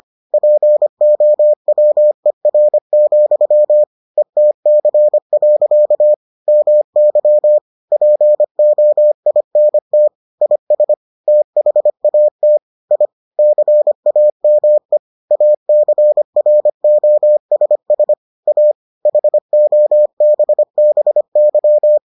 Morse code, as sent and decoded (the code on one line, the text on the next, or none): .--. --- .-- . .-. --..-- / . - -.-. .-.-.- / -- -.-- / .--. --- .. -. - / .. ... / - .... .- - / .. / -.-. .- -- . / .- -.-. .-. --- ... ... / .- / .... --- -... -... -.--
POWER, ETC. MY POINT IS THAT I CAME ACROSS A HOBBY